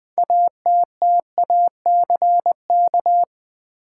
Morse code, heard as a keyed tone, about 20 wpm, 700 Hz. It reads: ATTACK
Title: Sentences from Top 1500 Words - New Word Drill Alt - 20wpm